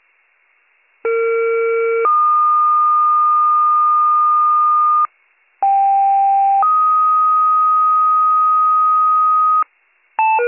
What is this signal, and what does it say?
Two-tone page, heard as a one-second tone, then a three-second tone, about 460 then 1220 Hz, about 790 then 1250 Hz. It is heard from 1.0 to 9.6 s.